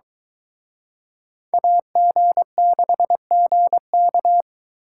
Morse code, heard as a keyed tone, about 23 wpm, 700 Hz. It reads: AG6GK